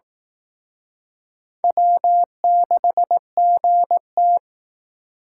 Morse code, heard as a keyed tone, about 20 wpm, 700 Hz. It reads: W6GT